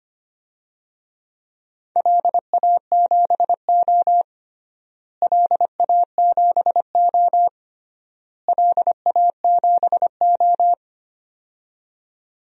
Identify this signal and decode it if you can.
Morse code, heard as a keyed tone, about 25 wpm, 700 Hz. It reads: LA7O LA7O LA7O
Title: DXCC-100 Call Signs - 4 Characters - Repeated 3x - 25wpm